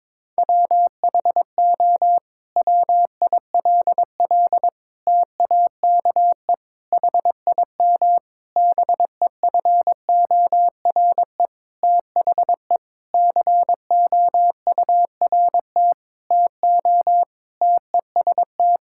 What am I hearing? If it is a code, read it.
Morse code, heard as a keyed tone, about 22 wpm, 700 Hz. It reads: WHO WILL TAKE HIM BEFORE THE COURT TO TEST
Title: Sentences from Top 1600 Words 22wpm